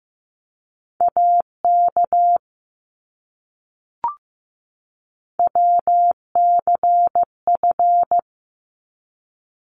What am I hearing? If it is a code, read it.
Morse code, heard as a keyed tone, about 15 wpm, 700 Hz. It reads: AK WCF
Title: ARRL Sections 15wpm